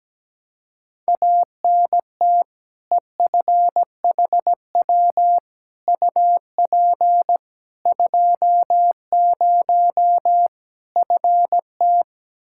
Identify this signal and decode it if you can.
Morse code, heard as a keyed tone, about 17 wpm, 700 Hz. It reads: ANT EFHW UP 20 FT
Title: Practice QSO - Exchange 2 - Info - 17wpm